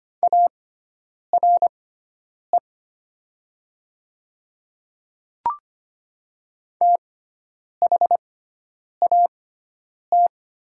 Morse code, heard as a keyed tone, about 25 wpm, spaced as for 9 wpm, 700 Hz. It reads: ARE THAT
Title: 6x Character Spacing - Top 100 Words - 25wpm